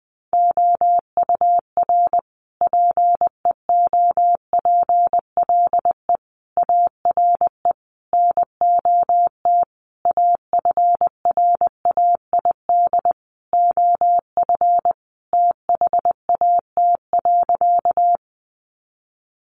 Morse code, heard as a keyed tone, about 20 wpm, 700 Hz. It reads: OUR PEOPLE ARE NOT AFRAID OF THAT.
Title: Sentences from Top 1500 Words 20wpm